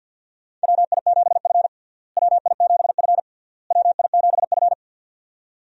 Morse code, heard as a keyed tone, about 53 wpm, 700 Hz. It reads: WI7F WI7F WI7F
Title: USA Call Signs - 4 Characters - Repeated 3x - 50wpm